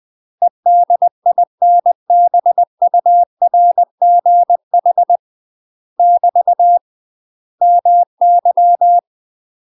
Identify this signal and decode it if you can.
Morse code, heard as a keyed tone, about 20 wpm, 700 Hz. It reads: EDINBURGH <BT> MY
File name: MASTER QSOs TRAINING/20WPM